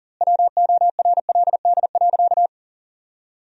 Morse code, heard as a keyed tone, about 37 wpm, 700 Hz. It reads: WORLD.
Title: 2x Word-Spacing - Sentences from Top 200 Words - 40wpm